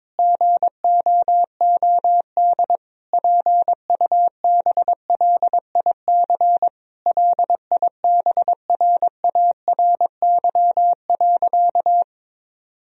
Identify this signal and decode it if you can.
Morse code, heard as a keyed tone, about 22 wpm, 700 Hz. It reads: GOOD PUBLIC LIBRARY.
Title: Sentences from Top 1000 Words - Review Part 4 - 22wpm